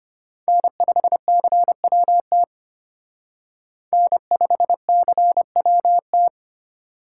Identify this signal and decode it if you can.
Morse code, heard as a keyed tone, about 29 wpm, 700 Hz. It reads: N5CWT N5CWT
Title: Speed Racing - USA 1x3 Call Signs 25wpm